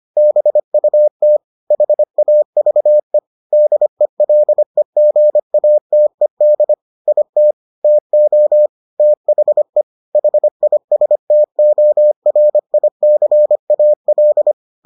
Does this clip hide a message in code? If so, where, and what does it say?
Yes: Morse code from 0.2 to 14.5 s, heard as a keyed tone, about 25 wpm, 600 Hz. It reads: BUT HAVE DELEGATED IT TO THE HISTORICAL